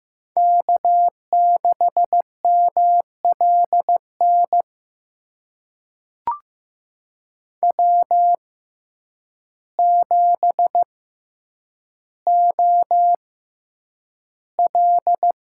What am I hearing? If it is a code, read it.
Morse code, heard as a keyed tone, about 15 wpm, spaced as for 8 wpm, 700 Hz. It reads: K6MLN W7OL
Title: USA Call Signs - 5 Characters - 6x Character Spacing - 15wpm